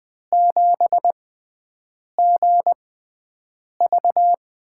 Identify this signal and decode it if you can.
Morse code, heard as a keyed tone, about 20 wpm, spaced as for 7 wpm, 700 Hz. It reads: 7GV
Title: USA Call Signs - 6x Character Spacing - 20wpm